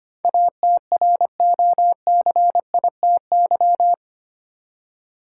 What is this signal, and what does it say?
Morse code, heard as a keyed tone, about 25 wpm, 700 Hz. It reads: ATROCITY
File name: 8 Letter Words - First 3 Letters Given - 25wpm